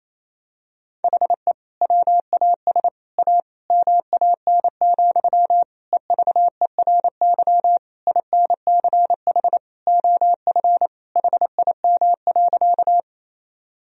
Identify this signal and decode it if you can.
Morse code, heard as a keyed tone, about 28 wpm, 700 Hz. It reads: HE WAS A MAN, EVERY INCH OF HIM.
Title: Sentences from Top 1000 Words - Review Part 3 - 28wpm